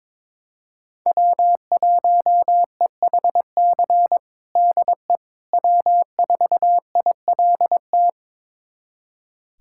Morse code, heard as a keyed tone, about 22 wpm, 700 Hz. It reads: W1EHC DE W4ILT
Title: USA Callsign DE Callsign - 5 Characters - 22wpm